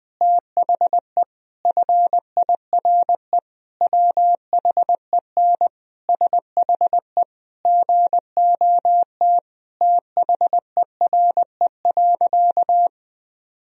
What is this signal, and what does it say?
Morse code, heard as a keyed tone, about 20 wpm, 700 Hz. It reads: THE FIRE WHEN SHE GOT THERE.
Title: Sentences from Top 1000 Words - Review Part 1 - 20wpm